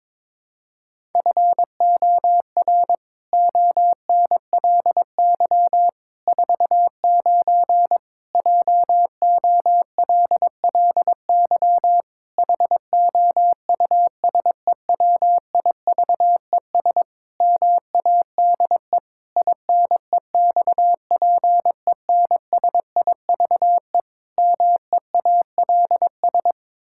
Morse code, heard as a keyed tone, about 22 wpm, 700 Hz. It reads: FOR ONLY 49 JOLLY HOUSEWIVES MADE INEXPENSIVE MEALS
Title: Pangram Challenge v2 - 22wpm